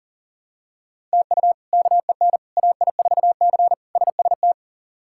Morse code, heard as a keyed tone, about 40 wpm, 690 Hz. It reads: TU KEN AI4C SST
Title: K1USN SST QSO Practice - 40wpm